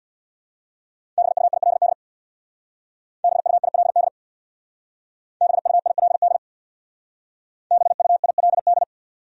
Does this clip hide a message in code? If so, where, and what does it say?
Yes: Morse code from 1.2 to 8.8 s, heard as a keyed tone, about 64 wpm, 700 Hz. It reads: BUILD BUILD BUILD BUILD